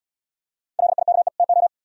Morse code, heard as a keyed tone, about 76 wpm, 700 Hz. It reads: LIKE TO